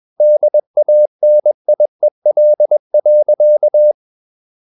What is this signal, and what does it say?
Morse code, heard as a keyed tone, about 21 wpm, 600 Hz. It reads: DANIEL.
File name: Ferie QRS 21wpm